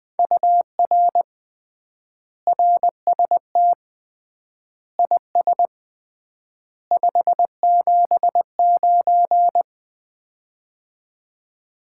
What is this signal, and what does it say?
Morse code, heard as a keyed tone, about 20 wpm, 700 Hz. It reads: UR RST IS 579
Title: Common QSO Phrases - 3x Word-Spacing - 20wpm